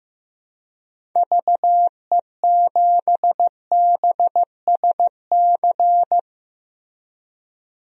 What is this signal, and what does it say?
Morse code, heard as a keyed tone, about 15 wpm, 700 Hz. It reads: VE7BSC
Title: DXCC-100 Call Signs - 6 Characters - Repeated 2x - 15wpm